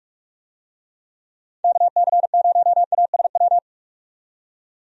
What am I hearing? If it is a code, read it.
Morse code, heard as a keyed tone, about 44 wpm, 700 Hz. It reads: KC0ASW
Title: USA Call Signs - 6 Characters - 45wpm